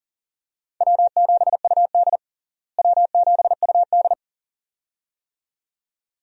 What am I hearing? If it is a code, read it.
Morse code, heard as a keyed tone, about 40 wpm, 700 Hz. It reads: W7UD W7UD
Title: USA Call Signs - 4 Characters - Repeated 2x - 40wpm